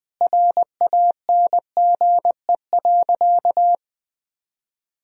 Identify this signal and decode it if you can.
Morse code, heard as a keyed tone, about 20 wpm, 700 Hz. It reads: RANGE.
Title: Sentences from Top 1000 Words - Review Part 4 - 20wpm